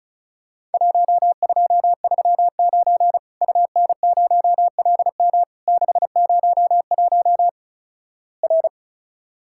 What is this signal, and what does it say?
Morse code, heard as a keyed tone, about 33 wpm, 700 Hz. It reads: 1239 UN0LM 601 R
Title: Worked All Europe DX Contest - QTC Exchange - 35wpm